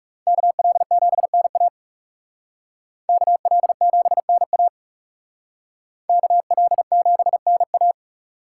Morse code, heard as a keyed tone, about 38 wpm, 700 Hz. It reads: KL7NA KL7NA KL7NA